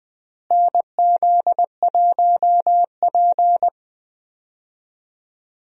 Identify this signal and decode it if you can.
Morse code, heard as a keyed tone, about 20 wpm, 700 Hz. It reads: NZ1P
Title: USA Call Signs - 4 Characters - Repeated 3x - 20wpm